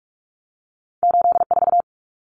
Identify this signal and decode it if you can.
Morse code, heard as a keyed tone, about 46 wpm, 700 Hz. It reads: 84
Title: Common Cut Numbers - 2-Digit 45wpm